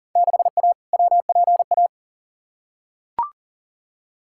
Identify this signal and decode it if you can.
Morse code, heard as a keyed tone, about 40 wpm, 700 Hz. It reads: 6A WPA
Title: Field Day Exchanges Encore - 40wpm